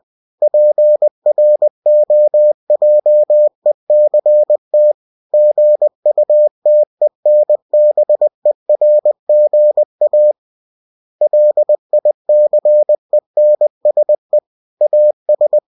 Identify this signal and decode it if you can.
Morse code, heard as a keyed tone, about 20 wpm, 600 Hz. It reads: PROJECT GUTENBERGA LICENSE AS
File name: cw-books / winnie-the-pooh / winnie-20wpm-page-0116